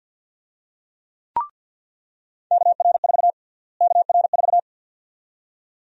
Morse code, heard as a keyed tone, about 46 wpm, 700 Hz. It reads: KR4 KR4
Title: USA Call Sign Prefixes - Repeated 2x - 50wpm